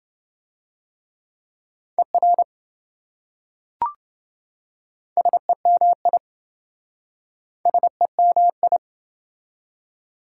Morse code, heard as a keyed tone, about 26 wpm, 700 Hz. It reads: ER SEMI SEMI